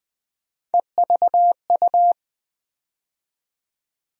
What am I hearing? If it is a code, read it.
Morse code, heard as a keyed tone, about 20 wpm, 700 Hz. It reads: EVU
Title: USA Call Sign Postfixes - 3 Letters 20wpm